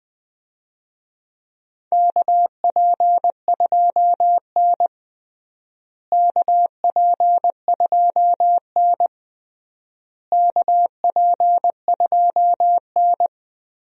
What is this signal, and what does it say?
Morse code, heard as a keyed tone, about 20 wpm, 700 Hz. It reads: KP2N KP2N KP2N
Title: DXCC-100 Call Signs - 4 Characters - Repeated 3x - 20wpm